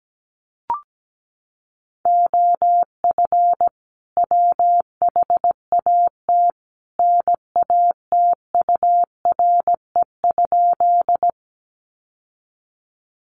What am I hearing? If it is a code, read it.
Morse code, heard as a keyed tone, about 17 wpm, 700 Hz. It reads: OF WHAT NATURE?
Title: Sentences from Top 600 Words 17wpm